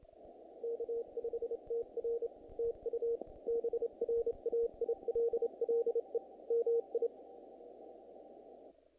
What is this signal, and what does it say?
Morse code, heard as a keyed tone, about 27 wpm, 470 Hz. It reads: K5TR TU BRAILLE MI
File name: contests / naqp-cw-k5tr-op-k5pi- -1800